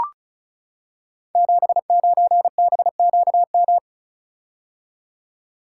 Morse code, heard as a keyed tone, about 36 wpm, 690 Hz. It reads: 79BQM
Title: Code Groups - 5 Letter:Numbers - 35wpm